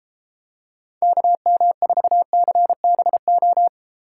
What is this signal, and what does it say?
Morse code, heard as a keyed tone, about 34 wpm, 700 Hz. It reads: KM4CBO